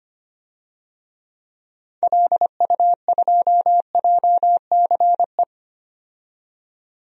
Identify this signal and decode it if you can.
Morse code, heard as a keyed tone, about 25 wpm, 700 Hz. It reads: LU2JCE
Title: DXCC-100 Call Signs - 6 Characters - 25wpm